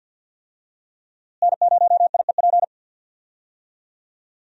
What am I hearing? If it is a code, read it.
Morse code, heard as a keyed tone, about 50 wpm, 690 Hz. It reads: N0IEP
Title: USA Call Signs - 5 Characters - Repeated 3x - 50wpm